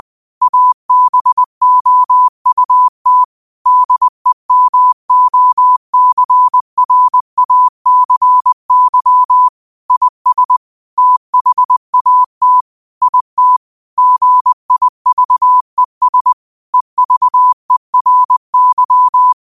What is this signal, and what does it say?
Morse code, heard as a keyed tone, about 20 wpm, 1000 Hz. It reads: ABOUT DEMOCRACY IS THAT IT GIVES EVERY
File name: qotdfiles / 20wpm